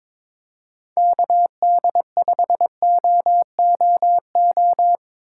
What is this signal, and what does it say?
Morse code, heard as a keyed tone, about 22 wpm, 700 Hz. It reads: KD5OOO
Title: USA Call Signs - 6 Characters - 8x Character Spacing - 22wpm